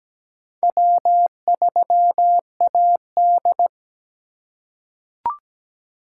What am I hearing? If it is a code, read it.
Morse code, heard as a keyed tone, about 17 wpm, 700 Hz. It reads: W3AD
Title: USA Call Signs - 4 Characters - 4x Character Spacing - 17wpm